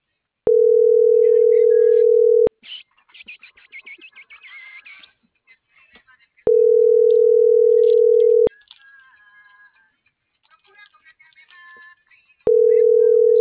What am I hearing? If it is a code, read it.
Ringback tone (North America), 440 + 480 Hz.